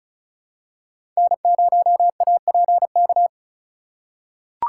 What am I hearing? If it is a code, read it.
Morse code, heard as a keyed tone, about 34 wpm, 700 Hz. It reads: N0APK
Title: USA Call Signs - 5 Characters - 3x Character Spacing - 35wpm